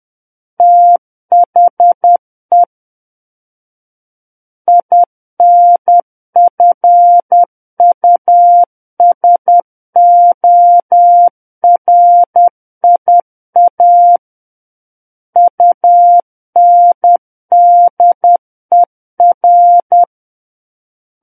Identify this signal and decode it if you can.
Morse code, heard as a keyed tone, about 10 wpm, 700 Hz. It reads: THE INFUSORIA UNDER
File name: Book One / 01-Chapter 1 - 10 WPM